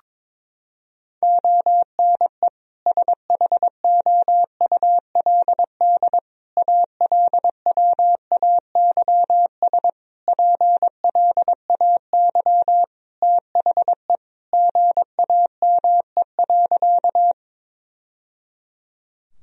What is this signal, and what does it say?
Morse code, heard as a keyed tone, about 22 wpm, 700 Hz. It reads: ONE SHOULD ALWAYS PLAY THE GAME.
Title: Sentences from Top 1000 Words - Review Part 3 - 22wpm